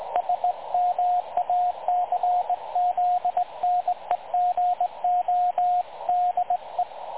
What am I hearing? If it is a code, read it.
Morse code, heard as a keyed tone, about 21 wpm, 720 Hz, with radio noise. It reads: SMACZNEGO DE
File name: dlazaawansowanych / nagraniazpasma / websdr recording 3565.0kHz